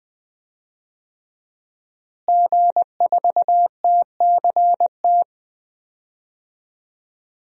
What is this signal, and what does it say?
Morse code, heard as a keyed tone, about 20 wpm, 700 Hz. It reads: G4TCT
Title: DXCC-100 Call Signs - 5 Characters - Repeated 2x - 20wpm